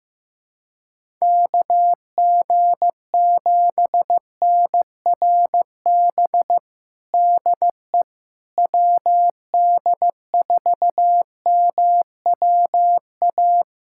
Morse code, heard as a keyed tone, about 15 wpm, 700 Hz. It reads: KG7NRB DE WD4MWA
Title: USA Callsign DE Callsign 15wpm